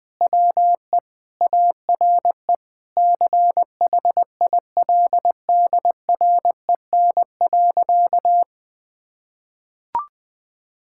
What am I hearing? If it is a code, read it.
Morse code, heard as a keyed tone, about 20 wpm, 700 Hz. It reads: WE ARE CHILDREN.
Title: Sentences from Top 1000 Words - Review Part 6 - 20wpm